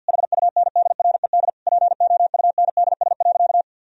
Morse code, heard as a keyed tone, about 50 wpm, 700 Hz. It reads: HUNDRED POUNDS.
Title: Sentences from Top 1000 Words - Review Part 2 - 50wpm